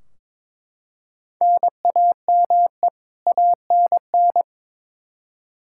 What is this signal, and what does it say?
Morse code, heard as a keyed tone, about 22 wpm, 700 Hz. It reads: NAME ANN
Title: K1USN SST QSO Practice 22wpm